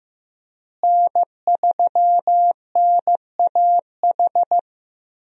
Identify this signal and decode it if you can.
Morse code, heard as a keyed tone, about 16 wpm, 700 Hz. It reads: N3NAH
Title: USA Call Signs - 5 Characters - Repeated 2x - 15wpm